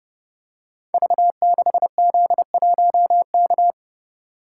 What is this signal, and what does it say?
Morse code, heard as a keyed tone, about 30 wpm, 700 Hz. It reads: V6Z1K
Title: Code Groups - 5 Letter:Numbers - 30wpm